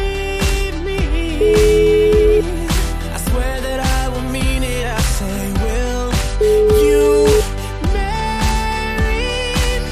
Ringback tone (Europe, 425 Hz), with noise.